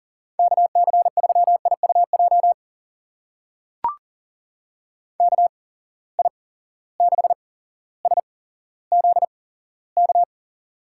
Morse code, heard as a keyed tone, about 40 wpm, spaced as for 17 wpm, 700 Hz. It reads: KC3IUJ KI6SZK
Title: USA Call Signs - 6 Characters - 8x Character Spacing - 40wpm